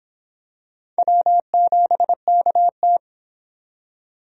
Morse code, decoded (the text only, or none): W7KT